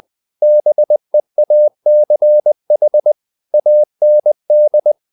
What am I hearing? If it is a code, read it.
Morse code, heard as a keyed tone, about 20 wpm, 600 Hz. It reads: BEACH AND
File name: the old man and the sea part 2 20wpm